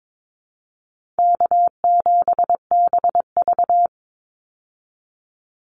Morse code, heard as a keyed tone, about 22 wpm, 700 Hz. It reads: K7BV
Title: Easy Call Signs Repeated 2x - 22wpm